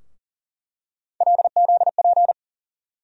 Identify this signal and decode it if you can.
Morse code, heard as a keyed tone, about 40 wpm, 700 Hz. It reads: LZP
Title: USA Call Sign Postfixes - 3 Letters 40wpm